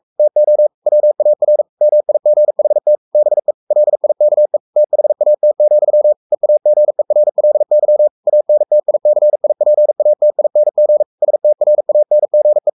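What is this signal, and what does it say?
Morse code, heard as a keyed tone, about 44 wpm, 600 Hz. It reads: TO WAR MIGHT BE LIKE THAT, EAGERLY ANTICIPATING STRANGE
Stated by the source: Morse code at 43 wpm sending Ken Follett's novel Never in English, from CW med Gnister.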